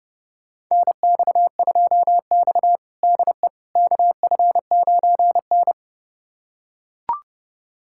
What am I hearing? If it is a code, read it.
Morse code, heard as a keyed tone, about 30 wpm, 700 Hz. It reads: NX2X DE KF9N
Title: USA Callsign DE Callsign - 4 Characters - 30wpm